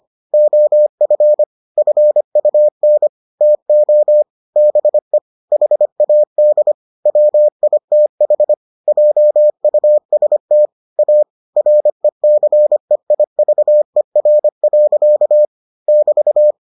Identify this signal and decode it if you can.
Morse code, heard as a keyed tone, about 25 wpm, 600 Hz. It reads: OF FUN TO BE HAD WITH JUST A RECEIVER. <BT>